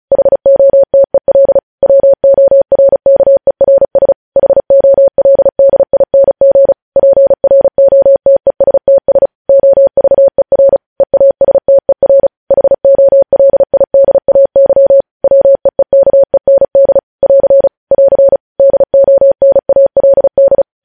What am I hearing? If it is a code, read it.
Morse code, heard as a keyed tone, about 36 wpm, 550 Hz. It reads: HOTEL WORKERS HOLDING PROTESTS OVER EASTER HOLIDAY WEEKEND <AR> <AR> DONALD
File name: podcast / News headlines-1x-35x35